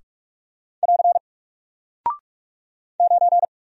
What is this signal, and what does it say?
Morse code, heard as a keyed tone, about 44 wpm, 700 Hz. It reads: <AR> 9
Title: Single Character (Letter:Number:Punctuation:Prosign) Alt 45wpm